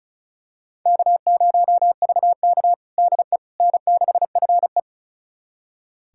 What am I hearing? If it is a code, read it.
Morse code, heard as a keyed tone, about 36 wpm, 700 Hz. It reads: K0VK DE N6FE
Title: USA Callsign DE Callsign - 4 Characters - 35wpm